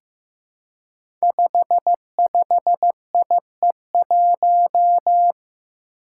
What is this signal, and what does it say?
Morse code, heard as a keyed tone, about 15 wpm, 700 Hz. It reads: 55IE1
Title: Five Challenging Letter-Number Combinations 15wpm